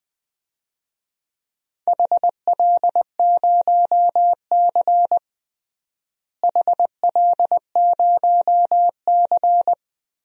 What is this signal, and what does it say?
Morse code, heard as a keyed tone, about 20 wpm, 700 Hz. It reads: HL0C HL0C
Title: DXCC-100 Call Signs - 4 Characters - Repeated 3x - 20wpm